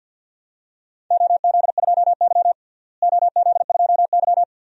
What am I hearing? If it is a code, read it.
Morse code, heard as a keyed tone, about 48 wpm, 700 Hz. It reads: OZ2Y OZ2Y
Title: DXCC-100 Call Signs - 4 Characters - Repeated 3x - 50wpm